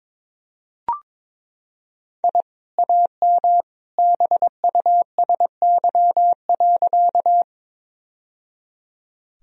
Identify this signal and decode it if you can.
Morse code, heard as a keyed tone, about 22 wpm, 700 Hz. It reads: I AM BUSY.
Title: Sentences from Top 1000 Words - Review Part 3 - 22wpm